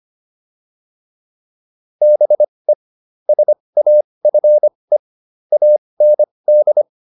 Morse code, heard as a keyed tone, about 25 wpm, spaced as for 20 wpm, 600 Hz. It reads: BE SAFE AND